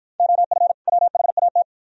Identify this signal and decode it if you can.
Morse code, heard as a keyed tone, about 53 wpm, 700 Hz. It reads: OF WHAT